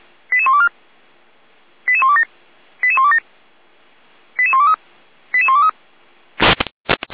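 Five-tone selective call (ZVEI), heard as quick runs of five tones, tones of about 70 ms. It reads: 88124 88127 88127 88123 88123
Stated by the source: ZVEI five-tone selective call 88191, received by SDR.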